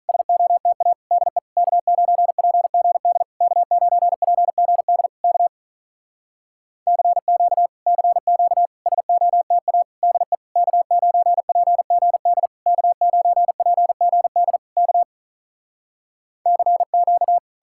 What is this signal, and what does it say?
Morse code, heard as a keyed tone, about 48 wpm, 700 Hz. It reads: SOTA DE K9PGD K9PGD K CQ CQ SOTA DE K9PGD K9PGD K CQ